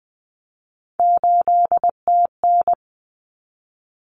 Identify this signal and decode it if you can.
Morse code, heard as a keyed tone, about 20 wpm, 700 Hz. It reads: 8TN
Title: Common Cut Numbers - 3-Digit 20wpm